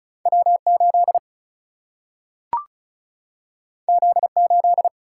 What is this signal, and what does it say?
Morse code, heard as a keyed tone, about 36 wpm, 700 Hz. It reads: W8 Z8
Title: Single-Letter Single-Number 35wpm